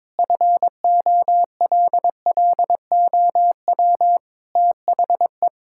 Morse code, heard as a keyed tone, about 22 wpm, 700 Hz. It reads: FOLLOW THE
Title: Sentences from Top 1600 Words 22wpm